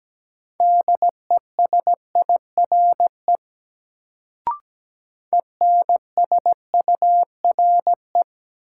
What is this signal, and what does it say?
Morse code, heard as a keyed tone, about 17 wpm, 700 Hz. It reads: DESIRE ENSURE
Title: Sentences from Top 1100 Words - New Word Drill Alt - 17wpm